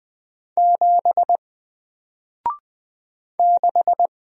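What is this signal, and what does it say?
Morse code, heard as a keyed tone, about 20 wpm, 700 Hz. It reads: 7 6
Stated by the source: Single letter and number Morse drill at 20 wpm, from N4LCD.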